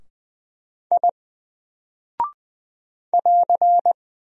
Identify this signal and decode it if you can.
Morse code, heard as a keyed tone, about 20 wpm, 700 Hz. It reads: I <AR>
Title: Single Character (Letter:Number:Punctuation:Prosign) 20wpm